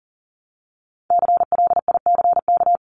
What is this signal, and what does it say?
Morse code, heard as a keyed tone, about 40 wpm, 700 Hz. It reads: CLICK
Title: Top 1001-1250 Words 40wpm